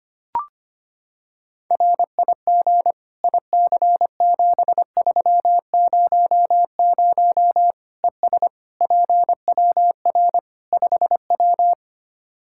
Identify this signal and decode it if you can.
Morse code, heard as a keyed tone, about 25 wpm, 700 Hz. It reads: RIG IC7300 ES PWR 5W
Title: Practice QSO - Exchange 2 - Info - 25wpm